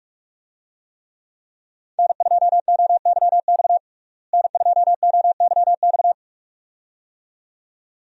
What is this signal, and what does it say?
Morse code, heard as a keyed tone, about 42 wpm, 700 Hz. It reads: N2OYX N2OYX
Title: USA Call Signs - 5 Characters - Repeated 2x - 45wpm